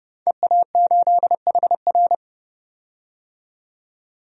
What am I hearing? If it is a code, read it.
Morse code, heard as a keyed tone, about 32 wpm, 700 Hz. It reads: EA8HR
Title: DXCC-100 Call Signs - 5 Characters - 30wpm